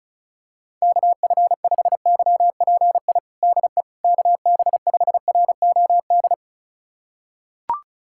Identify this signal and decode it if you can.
Morse code, heard as a keyed tone, about 34 wpm, 700 Hz. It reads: KF5YPI DE KB5ROD
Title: USA Callsign DE Callsign 35wpm